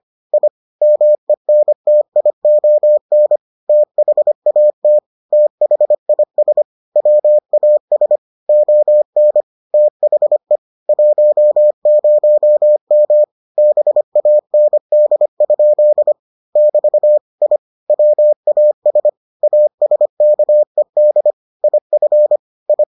Morse code, heard as a keyed tone, about 25 wpm, 600 Hz. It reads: I MENTION THAT THIS WAS ON THE 10M BAND? <BT> I WAS ASKED IF I